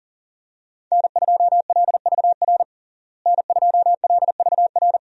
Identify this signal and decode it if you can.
Morse code, heard as a keyed tone, about 37 wpm, 700 Hz. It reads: N2LVR N2LVR
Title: USA Call Signs - 5 Characters - Repeated 3x - 40wpm